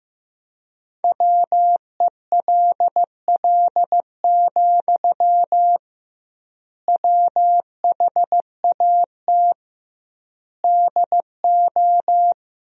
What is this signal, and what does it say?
Morse code, heard as a keyed tone, about 15 wpm, 700 Hz. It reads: WELL, WHAT DO
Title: 2x Word-Spacing - Sentences from Top 200 Words - 15wpm